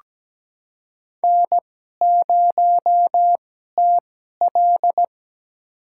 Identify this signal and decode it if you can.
Morse code, heard as a keyed tone, about 17 wpm, spaced as for 12 wpm, 700 Hz. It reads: N0TL
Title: USA Call Signs - 4 Characters - 2x Character Spacing - 17wpm